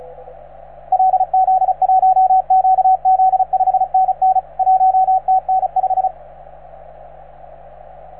Morse code, heard as a keyed tone, about 34 wpm, 750 Hz, with radio noise. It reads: LZ1QZ 5NN 1TN5